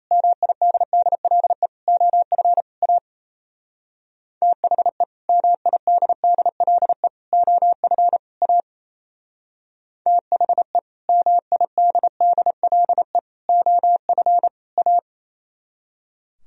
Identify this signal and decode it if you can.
Morse code, heard as a keyed tone, about 35 wpm, 700 Hz. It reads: MIDDLE OF A THE MIDDLE OF A THE MIDDLE OF A